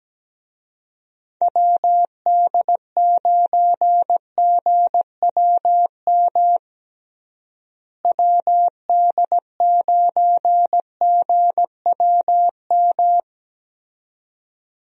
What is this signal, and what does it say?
Morse code, heard as a keyed tone, about 17 wpm, 700 Hz. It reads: WD9GWM WD9GWM
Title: USA Call Signs - 6 Characters - Repeated 2x - 17wpm